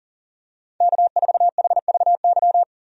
Morse code, heard as a keyed tone, about 37 wpm, 700 Hz. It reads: K4HVY
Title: USA Call Signs - 5 Characters - 40wpm